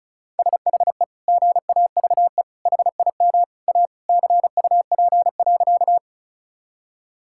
Morse code, heard as a keyed tone, about 36 wpm, 700 Hz. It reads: SHE GAVE HIM A CUP.
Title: Sentences from Top 1600 Words 35wpm